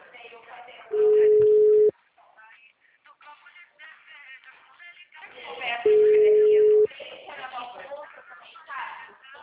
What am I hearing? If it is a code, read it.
Ringback tone (Europe, 425 Hz), with speech.